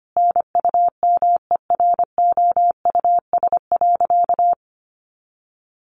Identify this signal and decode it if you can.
Morse code, heard as a keyed tone, about 25 wpm, 700 Hz. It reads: NUMEROUS.
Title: Sentences from Top 1400 Words 25wpm